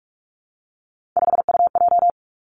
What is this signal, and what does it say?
Morse code, heard as a keyed tone, about 46 wpm, 700 Hz. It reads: 5UJ
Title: Three Challenging Letter-Number Combinations 45wpm